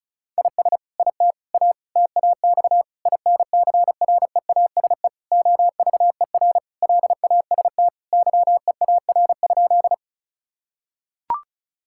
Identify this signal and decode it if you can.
Morse code, heard as a keyed tone, about 36 wpm, 700 Hz. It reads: IS IT A TAX INCREASE OVER LAST YEAR?
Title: Sentences from Top 1000 Words - Review Part 5 - 35wpm